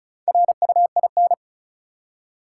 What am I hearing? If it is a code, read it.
Morse code, heard as a keyed tone, about 36 wpm, 700 Hz. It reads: RUIN
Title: Four Letter Words - 35wpm